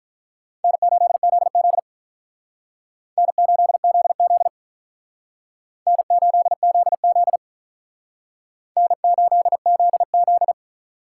Morse code, heard as a keyed tone, about 43 wpm, 700 Hz. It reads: N8ZZ N8ZZ N8ZZ N8ZZ
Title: Speed Racing - USA 1x2 Call Signs 35wpm